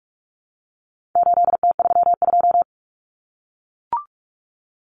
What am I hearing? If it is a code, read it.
Morse code, heard as a keyed tone, about 44 wpm, 700 Hz. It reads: 8T32
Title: Common Cut Numbers - 4-Digit 45wpm